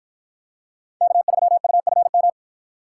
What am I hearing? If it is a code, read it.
Morse code, heard as a keyed tone, about 51 wpm, 700 Hz. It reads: K3UFM